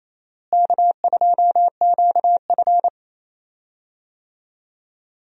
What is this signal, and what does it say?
Morse code, heard as a keyed tone, about 28 wpm, 700 Hz. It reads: K2QF
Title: USA Call Signs - 4 Characters - Repeated 2x - 28wpm